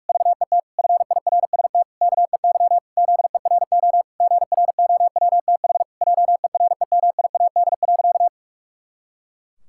Morse code, heard as a keyed tone, about 46 wpm, 700 Hz. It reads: VET FIRST KEY ZERO GROWTH JEREMIAD.
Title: Pangram Challenge v2 - 45wpm